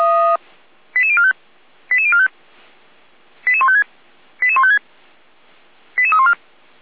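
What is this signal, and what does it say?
Five-tone selective call (ZVEI), heard as quick runs of five tones, tones of about 70 ms. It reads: 88045 88045 88156 88156 88314
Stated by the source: ZVEI five-tone selective call 88253 received with an SDR, from the feuerwehr-orlamuende fire brigade server.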